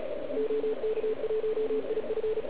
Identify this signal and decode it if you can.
Morse code, heard as a keyed tone, about 36 wpm, 440 Hz, with radio noise. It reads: OK1IF